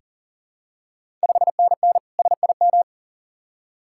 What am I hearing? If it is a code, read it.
Morse code, heard as a keyed tone, about 40 wpm, 690 Hz. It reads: 5NN SIM
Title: Ontario QSO Party Exchange 40wpm